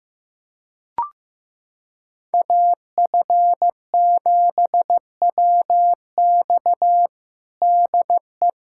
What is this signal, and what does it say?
Morse code, heard as a keyed tone, about 16 wpm, 700 Hz. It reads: AF7WX DE
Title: USA Callsign DE Callsign - 5 Characters - 15wpm